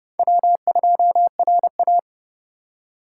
Morse code, heard as a keyed tone, about 29 wpm, 700 Hz. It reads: W2RA